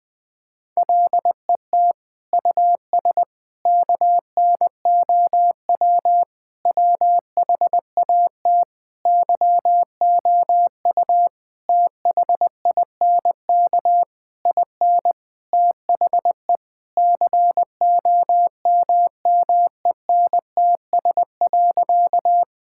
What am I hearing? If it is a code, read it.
Morse code, heard as a keyed tone, about 20 wpm, 700 Hz. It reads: LET US KNOW WHAT YOU THINK IN THE COMMENTS.
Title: Sentences with 8-Word Phrases - 20wpm